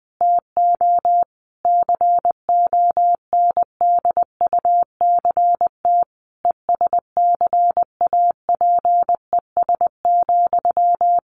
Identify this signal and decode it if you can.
Morse code, heard as a keyed tone, about 20 wpm, 700 Hz. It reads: TO CONDUCT ESCAPES,
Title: A Sound Of Thunder 20wpm